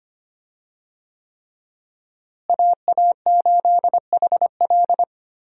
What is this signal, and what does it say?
Morse code, heard as a keyed tone, about 25 wpm, 700 Hz. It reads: AA8HL
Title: USA Call Signs - 5 Characters - Repeated 3x - 25wpm